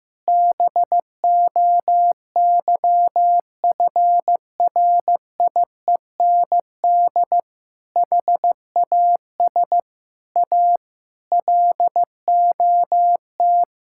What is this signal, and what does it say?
Morse code, heard as a keyed tone, about 15 wpm, 700 Hz. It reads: BOYFRIEND HAS A LOT
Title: Sentences from Top 1000 Words - Review Part 5 - 15wpm